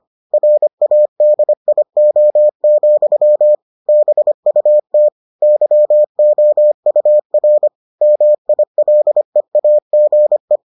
Morse code, heard as a keyed tone, about 25 wpm, 600 Hz. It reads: RADIO, BUT YOUR MILEAGE